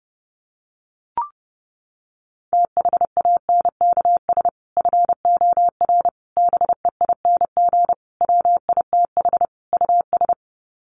Morse code, heard as a keyed tone, about 30 wpm, 700 Hz. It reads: THANKS FOR BEING WITH US
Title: Sets of 5 Words 30wpm